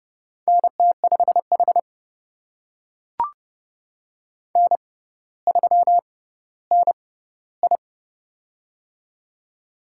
Morse code, heard as a keyed tone, about 30 wpm, spaced as for 15 wpm, 700 Hz. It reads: NT5H N3NI
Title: USA Call Signs - 4 Characters - 6x Character Spacing - 30wpm